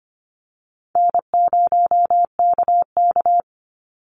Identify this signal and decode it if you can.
Morse code, heard as a keyed tone, about 25 wpm, 700 Hz. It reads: N0KK
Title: Easy Call Signs Repeated 3x - 25wpm